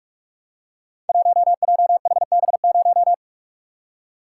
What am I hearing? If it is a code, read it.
Morse code, heard as a keyed tone, about 46 wpm, 700 Hz. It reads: 1JHB0
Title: Code Groups - 5 Letter:Numbers - 45wpm